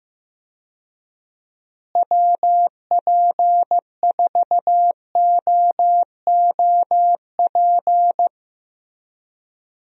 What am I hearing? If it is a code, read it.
Morse code, heard as a keyed tone, about 15 wpm, 700 Hz. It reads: WP4OOP
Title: USA Call Signs - 6 Characters - Repeated 3x - 15wpm